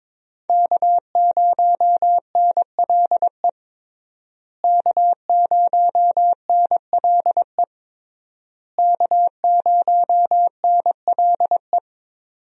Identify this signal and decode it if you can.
Morse code, heard as a keyed tone, about 22 wpm, 700 Hz. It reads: K0NLE K0NLE K0NLE
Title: USA Call Signs - 5 Characters - Repeated 3x - 22wpm